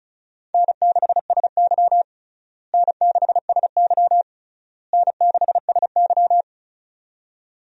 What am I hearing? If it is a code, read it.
Morse code, heard as a keyed tone, about 36 wpm, 700 Hz. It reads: N6SY N6SY N6SY
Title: USA Call Signs - 4 Characters - Repeated 3x - 35wpm